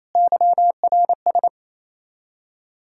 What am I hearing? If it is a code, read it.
Morse code, heard as a keyed tone, about 28 wpm, 700 Hz. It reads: YRS
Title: Practice QSO - Exchange 3 - More Info - 28wpm